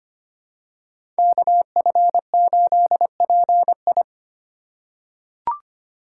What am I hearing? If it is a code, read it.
Morse code, heard as a keyed tone, about 25 wpm, 700 Hz. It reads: KF8PI
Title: USA Call Signs - 5 Characters - 4x Character Spacing - 25wpm